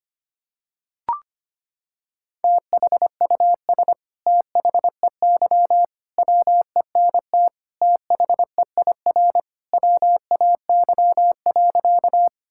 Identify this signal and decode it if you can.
Morse code, heard as a keyed tone, about 25 wpm, 700 Hz. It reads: THUS THEY WENT THEIR WAY.
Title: Sentences from Top 1000 Words - Review Part 5 - 25wpm